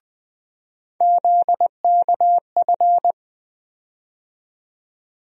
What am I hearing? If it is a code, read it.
Morse code, heard as a keyed tone, about 21 wpm, 700 Hz. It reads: ZKF
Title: USA Call Sign Postfixes - 3 Letters 20wpm